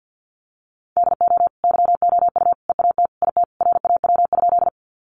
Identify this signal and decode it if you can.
Morse code, heard as a keyed tone, about 50 wpm, 700 Hz. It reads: DO YOU EAT IT RAW?
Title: Sentences from Top 1500 Words 50wpm